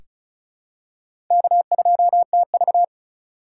Morse code, heard as a keyed tone, about 34 wpm, 700 Hz. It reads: K2TV
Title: K1USN SST QSO Practice - 35wpm